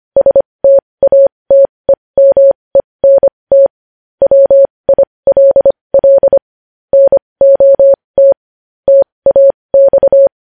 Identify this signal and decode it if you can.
Morse code, heard as a keyed tone, about 25 wpm, spaced as for 20 wpm, 550 Hz. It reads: STATEMENT WILL NOT TAX
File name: podcast / News headlines-1x-25x20